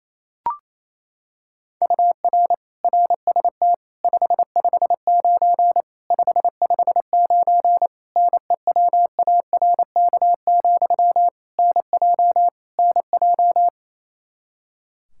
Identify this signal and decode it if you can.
Morse code, heard as a keyed tone, about 27 wpm, 700 Hz. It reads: UR RST 559 559 NEWARK, NJ NJ
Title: Practice QSO - Exchange 1 - Beginning - 28wpm